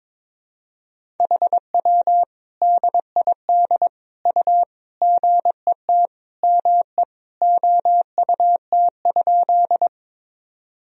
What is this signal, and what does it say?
Morse code, heard as a keyed tone, about 21 wpm, 700 Hz. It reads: HW DID U GET ME OUT?
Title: Sentences from Top 200 Words and Abbreviations 22wpm